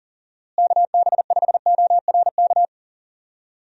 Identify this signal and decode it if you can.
Morse code, heard as a keyed tone, about 40 wpm, 700 Hz. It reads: KB5ORK
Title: USA Call Signs - 6 Characters - Repeated 2x - 40wpm